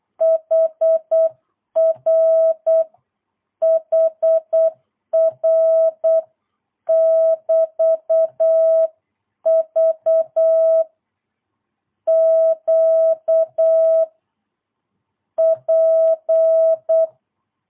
Morse code, heard as a keyed tone, about 8 wpm, 660 Hz. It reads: HR HR <BT> V Q P
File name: jh8cbh / 2022cw